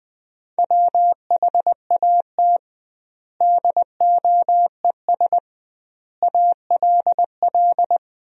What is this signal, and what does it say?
Morse code, heard as a keyed tone, about 20 wpm, 700 Hz. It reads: WHAT DOES ALL
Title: 2x Word-Spacing - Sentences from Top 200 Words - 20wpm